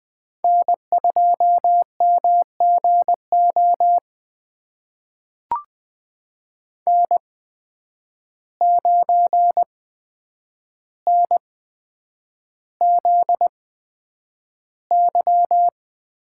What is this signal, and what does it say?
Morse code, heard as a keyed tone, about 20 wpm, spaced as for 9 wpm, 700 Hz. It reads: N2MGO N9NZY
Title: USA Call Signs - 8x Character Spacing - 20wpm